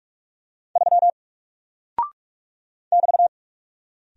Morse code, heard as a keyed tone, about 47 wpm, 700 Hz. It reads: 3 <BT>
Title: Single Character (Letter:Number:Punctuation:Prosign) Alt 45wpm